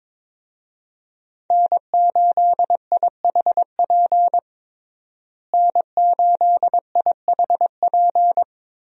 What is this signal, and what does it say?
Morse code, heard as a keyed tone, about 23 wpm, 700 Hz. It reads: N8IHP N8IHP
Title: USA Call Signs - 5 Characters - Repeated 3x - 22wpm